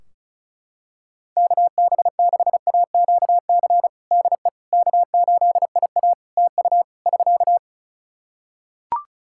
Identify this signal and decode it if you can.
Morse code, heard as a keyed tone, about 36 wpm, 700 Hz. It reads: KB6AQC DE K8IA TU <SK>
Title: K1USN SST QSO Practice 35wpm